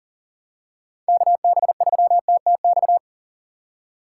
Morse code, heard as a keyed tone, about 37 wpm, 700 Hz. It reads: KB3TTX
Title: USA Call Signs - 6 Characters - 4x Character Spacing - 40wpm